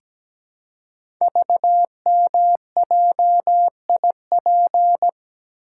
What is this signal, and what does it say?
Morse code, heard as a keyed tone, about 17 wpm, 700 Hz. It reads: VMJIP
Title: Code Groups - 5 Letter:Numbers - 17wpm